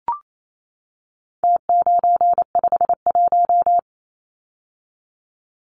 Morse code, heard as a keyed tone, about 28 wpm, 700 Hz. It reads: T951
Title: Numbers - Contest Style - 4-Digit 28wpm